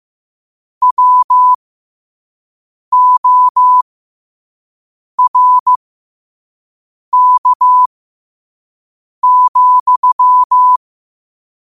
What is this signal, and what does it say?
Morse code, heard as a keyed tone, about 15 wpm, spaced as for 5 wpm, 1000 Hz. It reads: WORK,
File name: qotdfiles / 5wpm